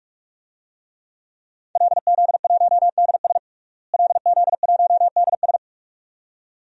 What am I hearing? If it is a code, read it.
Morse code, heard as a keyed tone, about 46 wpm, 700 Hz. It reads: LZ1DS LZ1DS
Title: DXCC-100 Call Signs - 5 Characters - Repeated 2x - 45wpm